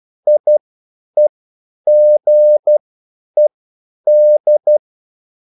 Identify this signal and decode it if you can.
Morse code, heard as a keyed tone, about 12 wpm, spaced as for 9 wpm, 600 Hz. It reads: IEGED